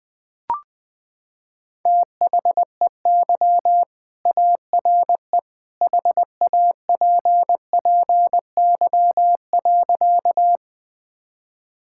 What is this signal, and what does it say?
Morse code, heard as a keyed tone, about 20 wpm, 700 Hz. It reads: THEY ARE HAPPY.
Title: Sentences with 2-Word Phrases - 20wpm